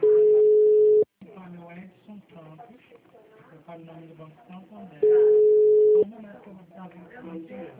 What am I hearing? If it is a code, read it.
Ringback tone (Europe, 425 Hz), with speech.
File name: out- -2020- - - .60945